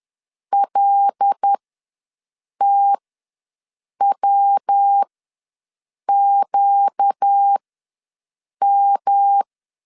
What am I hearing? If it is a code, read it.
Morse code, heard as a keyed tone, about 10 wpm, spaced as for 6 wpm, 800 Hz. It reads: LTWQM
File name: LEZIONI / 6 PKX0 / 6 PK ws 10wpm 3745